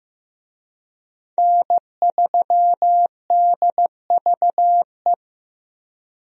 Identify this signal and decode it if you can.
Morse code, heard as a keyed tone, about 15 wpm, 700 Hz. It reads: N3DVE
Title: USA Call Signs - 5 Characters - Repeated 3x - 15wpm